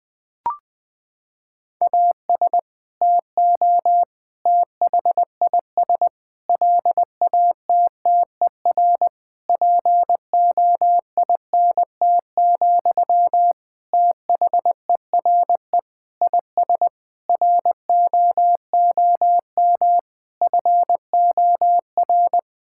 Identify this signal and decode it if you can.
Morse code, heard as a keyed tone, about 20 wpm, 700 Hz. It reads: AS TO THIS LATTER POINT, THERE IS ROOM FOR
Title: Sentences from Top 1600 Words 20wpm